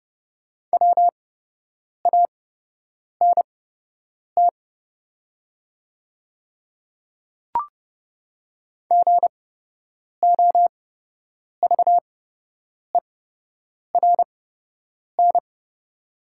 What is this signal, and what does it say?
Morse code, heard as a keyed tone, about 30 wpm, spaced as for 8 wpm, 700 Hz. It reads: WANT GOVERN
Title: 8x Character Spacing - Top 500 Words - 30wpm